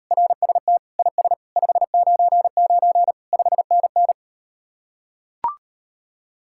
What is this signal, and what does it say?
Morse code, heard as a keyed tone, about 38 wpm, 700 Hz. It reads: RST IS 599 5NN